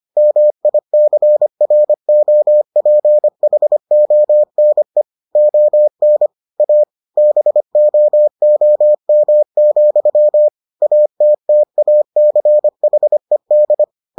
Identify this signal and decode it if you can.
Morse code, heard as a keyed tone, about 25 wpm, 600 Hz. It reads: MICROPHONE ON A BOOM, ATTACHED